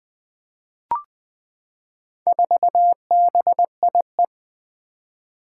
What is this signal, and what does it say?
Morse code, heard as a keyed tone, about 20 wpm, 700 Hz. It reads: 4BIE
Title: Four Challenging Letter-Number Combinations 20wpm